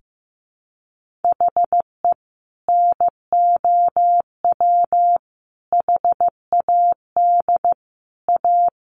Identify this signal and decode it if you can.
Morse code, heard as a keyed tone, about 15 wpm, 700 Hz. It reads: HE NOW HAD A
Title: Sentences from Top 1500 Words 15wpm